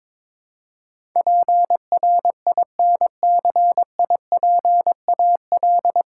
Morse code, heard as a keyed tone, about 22 wpm, 700 Hz. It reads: PRINCIPAL
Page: Morse Code Ninja practice 22wpm